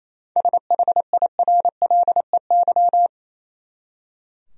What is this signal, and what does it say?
Morse code, heard as a keyed tone, about 28 wpm, 700 Hz. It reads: SHIRLEY
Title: USA Common Names - 28wpm